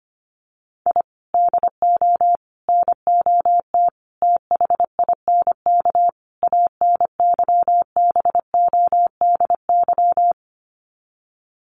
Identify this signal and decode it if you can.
Morse code, heard as a keyed tone, about 25 wpm, 700 Hz. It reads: I DO NOT THINK ANYBODY
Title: Sets of 5 Words 25wpm